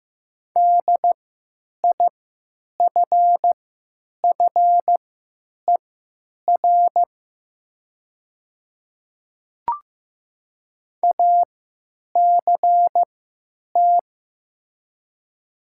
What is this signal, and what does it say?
Morse code, heard as a keyed tone, about 15 wpm, spaced as for 9 wpm, 700 Hz. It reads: DIFFER ACT
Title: 3x Character Spacing - Top 200 Words - 15wpm